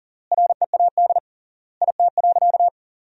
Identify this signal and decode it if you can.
Morse code, heard as a keyed tone, about 40 wpm, 700 Hz. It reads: READ IT.
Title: 3x Word-Spacing - Sentences from Top 500 Words - 40wpm